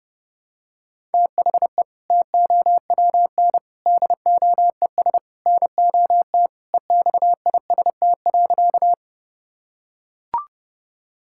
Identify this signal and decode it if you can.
Morse code, heard as a keyed tone, about 30 wpm, 700 Hz. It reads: THE TOWN DOES NOT EXIST.
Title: Sentences from Top 1000 Words - Review Part 5 - 30wpm